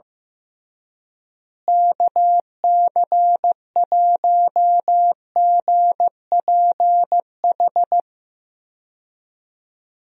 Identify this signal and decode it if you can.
Morse code, heard as a keyed tone, about 15 wpm, 700 Hz. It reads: KC1GPH
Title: USA Call Signs - 6 Characters - Repeated 3x - 15wpm